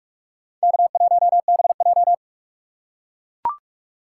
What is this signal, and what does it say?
Morse code, heard as a keyed tone, about 44 wpm, 700 Hz. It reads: K1BJ
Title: USA Call Signs - 4 Characters - 6x Character Spacing - 45wpm